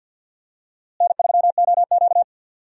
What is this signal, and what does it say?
Morse code, heard as a keyed tone, about 53 wpm, 700 Hz. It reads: N3OQ
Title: USA Call Signs - 4 Characters - Repeated 2x - 50wpm